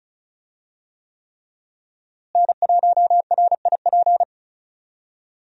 Morse code, heard as a keyed tone, about 36 wpm, 700 Hz. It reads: N1RIP
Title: USA Call Signs - 5 Characters - Repeated 3x - 35wpm